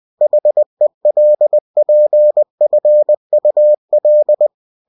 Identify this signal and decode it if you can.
Morse code, heard as a keyed tone, about 20 wpm, 600 Hz. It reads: HELPFUL
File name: cw-books / winnie-the-pooh / winnie-20wpm-page-0102